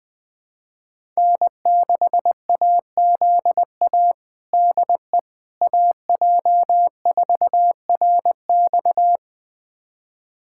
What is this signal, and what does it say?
Morse code, heard as a keyed tone, about 20 wpm, 700 Hz. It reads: N6AZA DE AJ4RX
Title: USA Callsign DE Callsign - 5 Characters - 20wpm